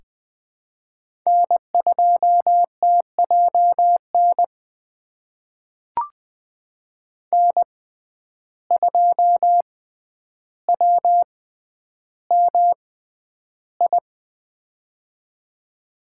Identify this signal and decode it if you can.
Morse code, heard as a keyed tone, about 20 wpm, spaced as for 10 wpm, 700 Hz. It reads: N2TJN N2WMI
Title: USA Call Signs - 5 Characters - 6x Character Spacing - 20wpm